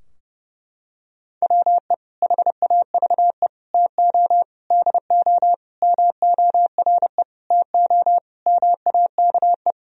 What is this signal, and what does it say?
Morse code, heard as a keyed tone, about 28 wpm, 700 Hz. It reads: WE HAVE TO DO MORE TO MAKE
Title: Sentences from Top 1000 Words - Review Part 3 - 30wpm